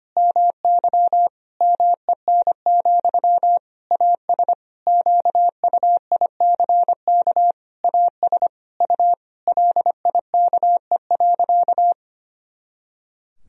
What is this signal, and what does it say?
Morse code, heard as a keyed tone, about 25 wpm, 700 Hz. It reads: MY MEN, AS QUICK AS U LIKE.
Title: Sentences from Top 500 Words and Abbreviations 25wpm